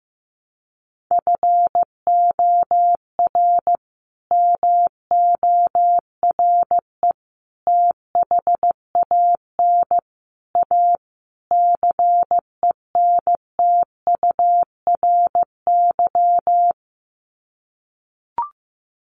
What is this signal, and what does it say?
Morse code, heard as a keyed tone, about 15 wpm, 700 Hz. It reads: FOR MORE THAN A CENTURY
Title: Sets of 5 Words Encore2 15wpm